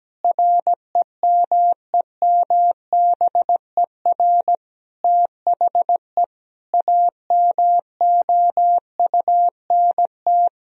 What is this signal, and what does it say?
Morse code, heard as a keyed tone, about 17 wpm, 700 Hz. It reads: REMEMBER THE AMOUNT
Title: Sentences from Top 1000 Words - Review Part 4 - 17wpm